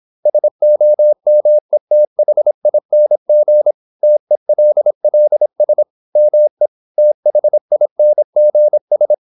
Morse code, heard as a keyed tone, about 26 wpm, 600 Hz. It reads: SOMETHING TELLS ME THINGS
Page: Never 26wpm | CW med Gnister